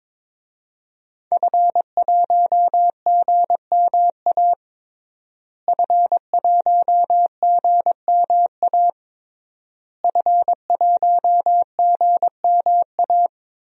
Morse code, heard as a keyed tone, about 22 wpm, 700 Hz. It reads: F1GMA F1GMA F1GMA
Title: DXCC-100 Call Signs - 5 Characters - Repeated 3x - 22wpm